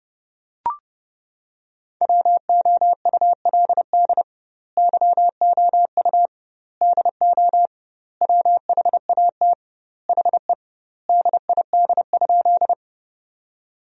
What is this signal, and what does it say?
Morse code, heard as a keyed tone, about 32 wpm, 700 Hz. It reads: WOULD YOU DO WHAT HE DID?
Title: 2x Word-Spacing - Sentences from Top 100 Words - 30wpm